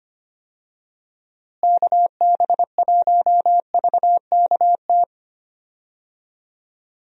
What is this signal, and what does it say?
Morse code, heard as a keyed tone, about 25 wpm, 700 Hz. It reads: KB1VKT
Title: USA Call Signs - 6 Characters - Repeated 3x - 25wpm